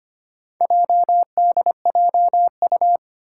Morse code, heard as a keyed tone, about 25 wpm, 700 Hz. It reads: JDJU
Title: Four Challenging Letter-Number Combinations 25wpm